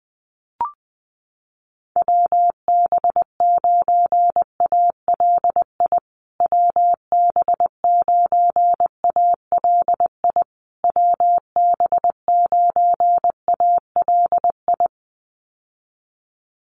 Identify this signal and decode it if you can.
Morse code, heard as a keyed tone, about 20 wpm, 700 Hz. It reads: WB9ALI WB9ALI WB9ALI
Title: USA Call Signs Repeated 3x 20wpm